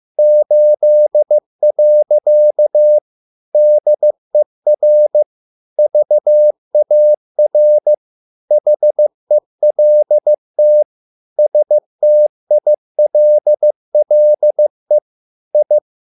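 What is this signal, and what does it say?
Morse code, heard as a keyed tone, about 15 wpm, 600 Hz. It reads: 8. DER VAR HELT STILLE I
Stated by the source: Morse code at a slow 15 wpm from CW med Gnister.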